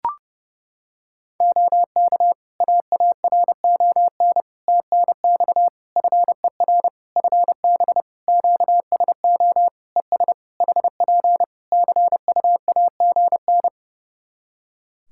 Morse code, heard as a keyed tone, about 30 wpm, 700 Hz. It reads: OK AARON TNX FER FB QSO ES HP CUAGN
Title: K1USN SST QSO Practice 30wpm